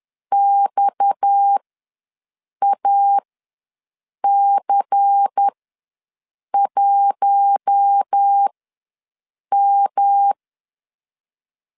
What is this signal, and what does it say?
Morse code, heard as a keyed tone, about 10 wpm, spaced as for 6 wpm, 800 Hz. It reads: XAC1M
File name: LEZIONI / 6 PKX0 / 6 PKX ws 10wpm 3706